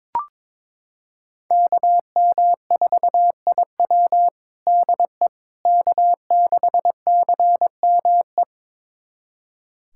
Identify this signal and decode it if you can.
Morse code, heard as a keyed tone, about 22 wpm, 700 Hz. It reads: KM4IW DE K6CME
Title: USA Callsign DE Callsign - 5 Characters - 22wpm